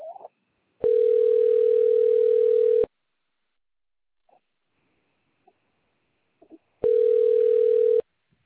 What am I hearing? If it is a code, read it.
Ringback tone (North America), 440 + 480 Hz.